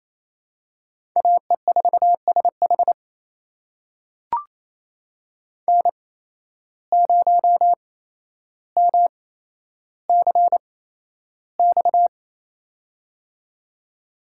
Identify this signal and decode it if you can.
Morse code, heard as a keyed tone, about 28 wpm, spaced as for 12 wpm, 700 Hz. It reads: AE4SH N0MCX
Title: USA Call Signs - 5 Characters - 8x Character Spacing - 28wpm